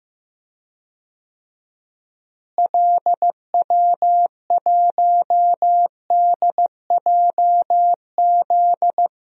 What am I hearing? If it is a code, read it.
Morse code, heard as a keyed tone, about 14 wpm, 700 Hz. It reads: LW1DJZ
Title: DXCC-100 Call Signs - 6 Characters - Repeated 2x - 15wpm